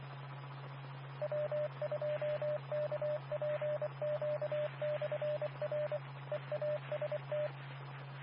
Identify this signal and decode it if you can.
Morse code, heard as a keyed tone, about 24 wpm, 600 Hz. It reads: W2KPQ/R EAST
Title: Nassau and Suffolk County Amateur Repeaters